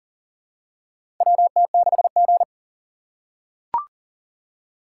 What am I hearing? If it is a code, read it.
Morse code, heard as a keyed tone, about 40 wpm, 700 Hz. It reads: WT6G
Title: USA Call Signs - 4 Characters - 3x Character Spacing - 40wpm